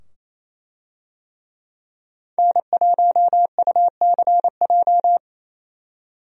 Morse code, heard as a keyed tone, about 28 wpm, 700 Hz. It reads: N1UCJ
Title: USA Call Signs - 5 Characters - 28wpm